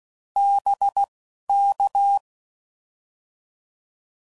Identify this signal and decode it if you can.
Morse code, heard as a keyed tone, about 16 wpm, 800 Hz. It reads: B K